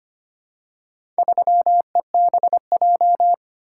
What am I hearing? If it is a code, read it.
Morse code, heard as a keyed tone, about 25 wpm, 700 Hz. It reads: 3EBJ
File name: practice / 25wpm / Four Letter-Number Combinations - 25wpm